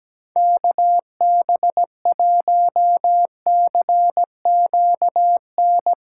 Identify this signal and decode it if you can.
Morse code, heard as a keyed tone, about 17 wpm, 700 Hz. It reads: KB1CQN
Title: USA Callsign DE Callsign 17wpm